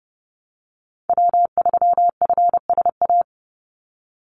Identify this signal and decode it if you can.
Morse code, heard as a keyed tone, about 30 wpm, 700 Hz. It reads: W3FSA
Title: Call Signs Repeated 3x - 30wpm